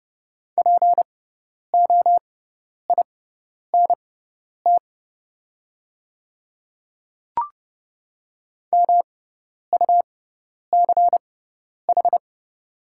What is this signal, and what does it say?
Morse code, heard as a keyed tone, about 30 wpm, spaced as for 10 wpm, 700 Hz. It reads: POINT MUCH
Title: 6x Character Spacing - Top 200 Words - 30wpm